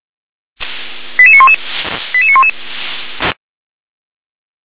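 Five-tone selective call (ZVEI), heard as quick runs of five tones, tones of about 70 ms. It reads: 88911 88911
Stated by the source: ZVEI five-tone call received via SDR.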